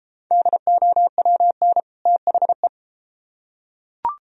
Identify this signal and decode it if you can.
Morse code, heard as a keyed tone, about 29 wpm, 700 Hz. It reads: DOWN THE